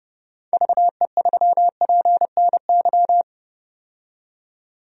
Morse code, heard as a keyed tone, about 30 wpm, 700 Hz. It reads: VE3PNY
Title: DXCC-100 Call Signs - 6 Characters - 30wpm